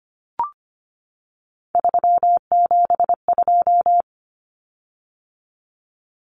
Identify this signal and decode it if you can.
Morse code, heard as a keyed tone, about 25 wpm, 700 Hz. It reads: 372
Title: Common Cut Numbers - 3-Digit 25wpm